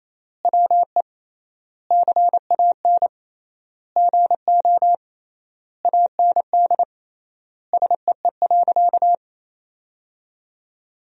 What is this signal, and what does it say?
Morse code, heard as a keyed tone, about 27 wpm, 700 Hz. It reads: WE CAN GO AND SEE.
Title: 3x Word-Spacing - Sentences from Top 100 Words - 28wpm